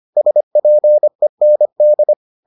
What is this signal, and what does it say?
Morse code, heard as a keyed tone, about 25 wpm, 600 Hz. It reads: SPEND